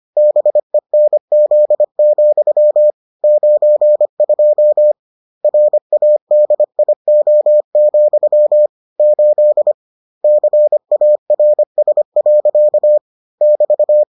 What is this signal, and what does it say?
Morse code, heard as a keyed tone, about 25 wpm, 600 Hz. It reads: BENZ, 92 RADIO, 8 CARS. <BT>